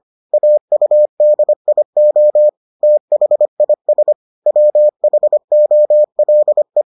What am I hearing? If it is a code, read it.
Morse code, heard as a keyed tone, about 25 wpm, 600 Hz. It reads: AUDIO THIS WHOLE